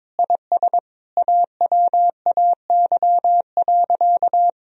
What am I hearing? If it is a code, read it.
Morse code, heard as a keyed tone, about 22 wpm, 700 Hz. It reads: IS AWAY.
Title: Sentences from Top 1000 Words - Review Part 2 - 22wpm